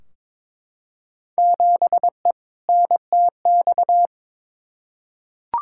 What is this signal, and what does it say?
Morse code, heard as a keyed tone, about 22 wpm, 700 Hz. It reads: 7E NTX
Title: Field Day Exchanges Encore - 22wpm